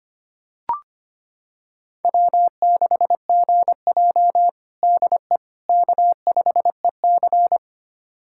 Morse code, heard as a keyed tone, about 25 wpm, 700 Hz. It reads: W6GJ DE K5EC
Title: USA Callsign DE Callsign - 4 Characters - 25wpm